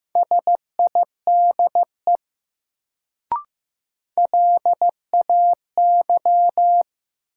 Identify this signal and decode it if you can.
Morse code, heard as a keyed tone, about 15 wpm, 700 Hz. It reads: SIDE LAY
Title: Sentences from Top 400 Words - New Word Drill - 15wpm